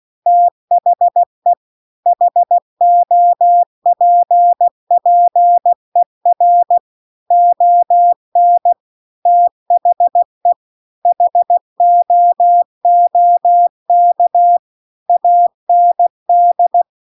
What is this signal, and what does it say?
Morse code, heard as a keyed tone, about 16 wpm, 700 Hz. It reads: THE HOPPER ON THE HOOK AND
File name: CW-The-Big-Twoheaded-River-2-16WPM